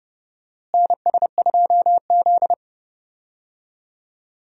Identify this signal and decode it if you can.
Morse code, heard as a keyed tone, about 31 wpm, 700 Hz. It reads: NS2Z
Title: USA Call Signs - 4 Characters - Repeated 2x - 30wpm